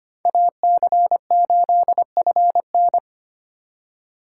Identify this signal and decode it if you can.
Morse code, heard as a keyed tone, about 33 wpm, 700 Hz. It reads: AC8FN